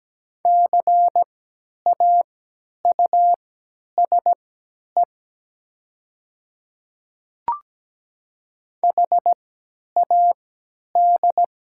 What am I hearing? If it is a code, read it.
Morse code, heard as a keyed tone, about 17 wpm, spaced as for 10 wpm, 690 Hz. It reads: CAUSE HAD
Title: 3x Character Spacing - Top 200 Words - 17wpm